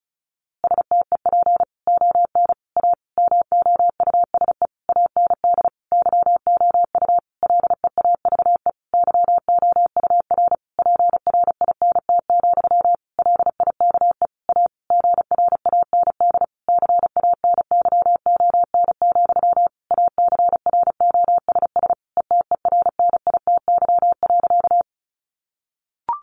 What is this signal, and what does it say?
Morse code, heard as a keyed tone, about 34 wpm, 700 Hz. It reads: STEP ON A MOUSE AND YOU LEAVE YOUR PRINT, LIKE A GRAND CANYON, ACROSS ETERNITY.
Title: A Sound Of Thunder 35wpm